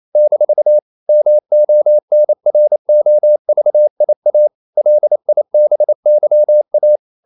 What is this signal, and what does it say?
Morse code, heard as a keyed tone, about 28 wpm, 600 Hz. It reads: - MONROVIA LIBYA
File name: CW / Practice / NationalCapitals28